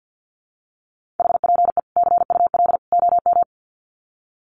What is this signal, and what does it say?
Morse code, heard as a keyed tone, about 48 wpm, 700 Hz. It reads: HPE CUL OM
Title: Common QSO Phrases 50wpm